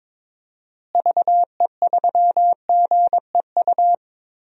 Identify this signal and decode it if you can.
Morse code, heard as a keyed tone, about 22 wpm, 700 Hz. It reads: VE3GEU
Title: DXCC-100 Call Signs - 6 Characters - Repeated 3x - 22wpm